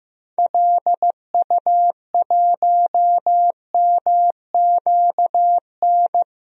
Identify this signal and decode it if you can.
Morse code, heard as a keyed tone, about 15 wpm, 700 Hz. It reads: LU1MQN
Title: DXCC-100 Call Signs - 6 Characters - 15wpm